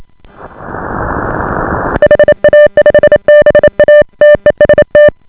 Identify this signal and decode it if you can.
Morse code, heard as a keyed tone, about 28 wpm, 590 Hz. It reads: HA5BA TEST